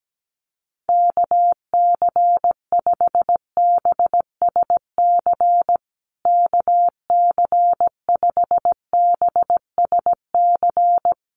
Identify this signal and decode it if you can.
Morse code, heard as a keyed tone, about 17 wpm, 700 Hz. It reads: KC5BSC KC5BSC
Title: USA Call Signs Repeated 3x 17wpm